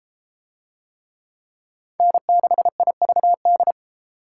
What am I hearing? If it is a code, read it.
Morse code, heard as a keyed tone, about 31 wpm, 700 Hz. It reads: N6IVD